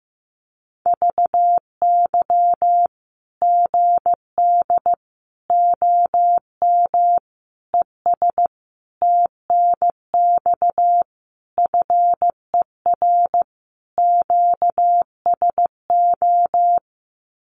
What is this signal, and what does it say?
Morse code, heard as a keyed tone, about 15 wpm, 700 Hz. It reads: VY GD OM ES TNX FER QSO
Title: Common QSO Phrases 15wpm